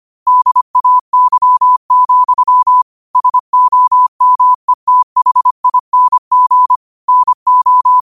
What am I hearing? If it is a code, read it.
Morse code, heard as a keyed tone, about 25 wpm, 1000 Hz. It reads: DAY, SOMETHING NO